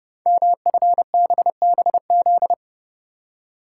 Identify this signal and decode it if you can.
Morse code, heard as a keyed tone, about 28 wpm, 700 Hz. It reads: MFBBZ
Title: Code Groups - 5 Letters - 30wpm